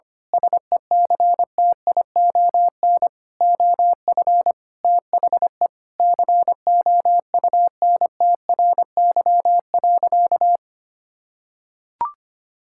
Morse code, heard as a keyed tone, about 25 wpm, 700 Hz. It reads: SECTION OF THE COUNTRY.
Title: Sentences from Top 1000 Words - Review Part 5 - 25wpm